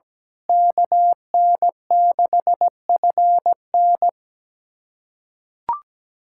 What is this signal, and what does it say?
Morse code, heard as a keyed tone, about 17 wpm, 700 Hz. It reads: KN6FN
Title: USA Callsign DE Callsign - 5 Characters - 17wpm